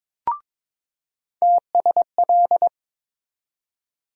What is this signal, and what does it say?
Morse code, heard as a keyed tone, about 22 wpm, 700 Hz. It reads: TSL
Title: USA Call Sign Postfixes - 3 Letters 22wpm